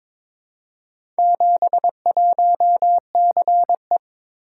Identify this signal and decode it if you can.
Morse code, heard as a keyed tone, about 22 wpm, 700 Hz. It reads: 71CE
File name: practice / 22wpm / Four Letter-Number Combinations - 22wpm